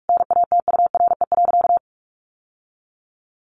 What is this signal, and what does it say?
Morse code, heard as a keyed tone, about 45 wpm, 700 Hz. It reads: NATURE.
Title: Sentences from Top 1500 Words 45wpm